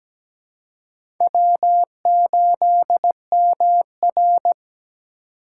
Morse code, heard as a keyed tone, about 17 wpm, 700 Hz. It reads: W8MR
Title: USA Call Signs - 4 Characters - 3x Character Spacing - 17wpm